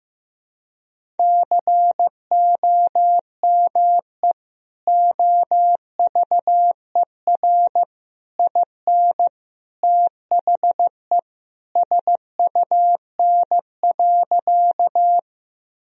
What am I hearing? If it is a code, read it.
Morse code, heard as a keyed tone, about 15 wpm, 700 Hz. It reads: COME OVER IN THE SUN.
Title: Sentences from Top 1000 Words - Review Part 1 - 15wpm